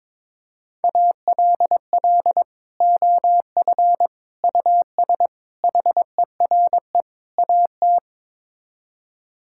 Morse code, heard as a keyed tone, about 22 wpm, 700 Hz. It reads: ALL OF US HERE AT
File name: practice / 22wpm / Build A Phrase - 5-Word Phrases - 22wpm